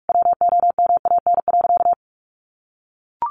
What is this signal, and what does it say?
Morse code, heard as a keyed tone, about 44 wpm, 700 Hz. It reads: WOMAN.
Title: Sentences from Top 1100 Words 45wpm